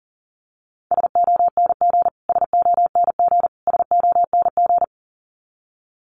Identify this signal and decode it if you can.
Morse code, heard as a keyed tone, about 40 wpm, 700 Hz. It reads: SONG SONG SONG
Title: Top 500 Words Repeated 40wpm